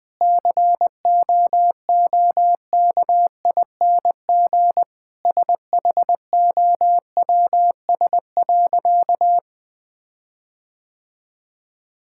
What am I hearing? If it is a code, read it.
Morse code, heard as a keyed tone, about 20 wpm, 700 Hz. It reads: COOKING SHOWS.
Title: Difficult Sentences with Hints - 20wpm